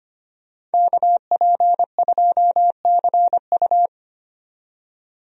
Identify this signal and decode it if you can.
Morse code, heard as a keyed tone, about 25 wpm, 700 Hz. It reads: KP2CU
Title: DXCC-100 Call Signs - 5 Characters - 25wpm